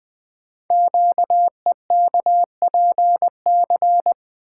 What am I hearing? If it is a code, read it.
Morse code, heard as a keyed tone, about 20 wpm, 700 Hz. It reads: QEKPC
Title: Code Groups - 5 Letters - 20wpm